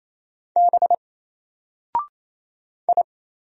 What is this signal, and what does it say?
Morse code, heard as a keyed tone, about 27 wpm, 700 Hz. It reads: B I
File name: Single-Letter-Number-28wpm